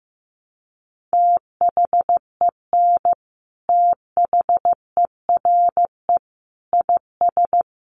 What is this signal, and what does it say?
Morse code, heard as a keyed tone, about 15 wpm, 700 Hz. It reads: THEN THERE IS
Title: Sentences from Top 100 Words 15wpm